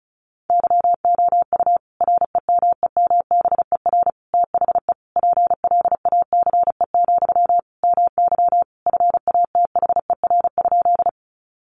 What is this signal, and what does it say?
Morse code, heard as a keyed tone, about 34 wpm, 700 Hz. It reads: YOU REMEMBER THE PLACE, MY FATHER?
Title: Sentences from Top 400 Words 35wpm